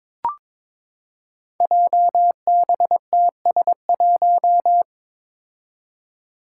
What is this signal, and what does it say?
Morse code, heard as a keyed tone, about 22 wpm, 700 Hz. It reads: JBTS1
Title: Five Challenging Letter-Number Combinations 22wpm